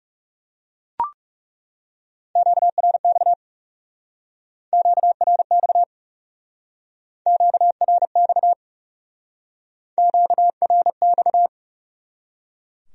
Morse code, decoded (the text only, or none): QRX QRX QRX QRX